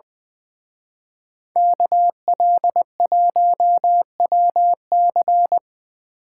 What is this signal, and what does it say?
Morse code, heard as a keyed tone, about 24 wpm, 700 Hz. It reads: KL1WC
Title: Speed Racing - USA 2x2 Call Signs 20wpm